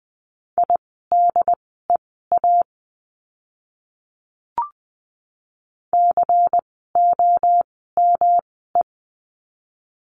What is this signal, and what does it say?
Morse code, heard as a keyed tone, about 20 wpm, spaced as for 14 wpm, 700 Hz. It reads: IDEA COME
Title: 108 Lesson 18f - Letter m - Words 20wpm